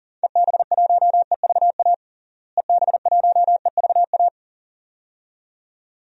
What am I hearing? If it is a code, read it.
Morse code, heard as a keyed tone, about 40 wpm, 700 Hz. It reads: EB1EVA EB1EVA
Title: DXCC-100 Call Signs - 6 Characters - Repeated 2x - 40wpm